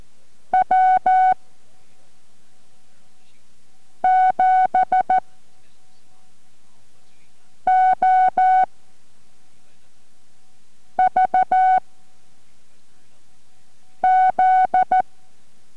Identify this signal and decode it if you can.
Morse code, heard as a keyed tone, about 14 wpm, 750 Hz. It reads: W 7 O V Z